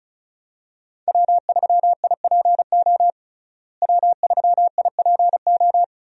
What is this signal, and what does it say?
Morse code, heard as a keyed tone, about 34 wpm, 700 Hz. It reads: W3IPO W3IPO
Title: USA Call Signs - 5 Characters - Repeated 3x - 35wpm